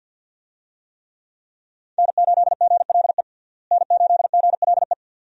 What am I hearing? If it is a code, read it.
Morse code, heard as a keyed tone, about 51 wpm, 700 Hz. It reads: N8GLE N8GLE
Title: USA Call Signs - 5 Characters - Repeated 2x - 50wpm